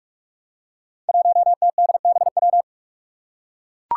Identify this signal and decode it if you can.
Morse code, heard as a keyed tone, about 44 wpm, 700 Hz. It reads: 1TDBW
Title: Five Challenging Letter-Number Combinations 45wpm